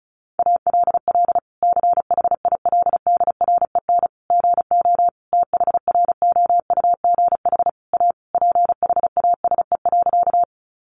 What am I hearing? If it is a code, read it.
Morse code, heard as a keyed tone, about 34 wpm, 700 Hz. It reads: ALL CHILDREN GO THROUGH A PHASE.
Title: Sentences from Top 1200 Words 35wpm